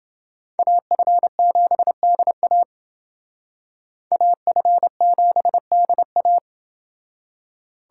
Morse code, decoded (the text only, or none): AF7DA AF7DA